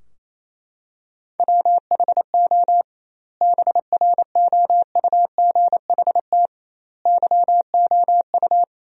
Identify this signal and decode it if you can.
Morse code, heard as a keyed tone, about 28 wpm, 700 Hz. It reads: WHO BROUGHT YOU
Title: 2x Word-Spacing - Sentences from Top 500 Words - 28wpm